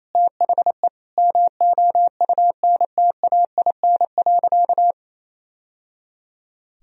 Morse code, heard as a keyed tone, about 28 wpm, 700 Hz. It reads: THE MOUNTAIN.
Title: Sentences from Top 1000 Words - Review Part 1 - 28wpm